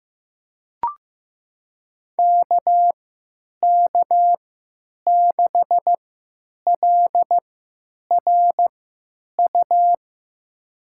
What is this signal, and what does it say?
Morse code, heard as a keyed tone, about 16 wpm, spaced as for 9 wpm, 700 Hz. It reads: KK6LRU
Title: USA Call Signs - 6 Characters - 3x Character Spacing - 15wpm